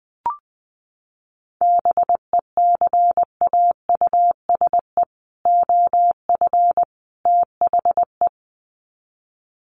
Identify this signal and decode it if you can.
Morse code, heard as a keyed tone, about 19 wpm, 700 Hz. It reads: BECAUSE OF THE
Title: Sets of 3 Words 20wpm